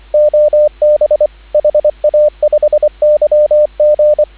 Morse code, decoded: OB HA5YG

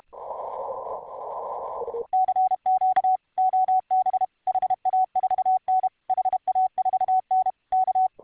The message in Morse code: CQ OB HA4N HA4N K

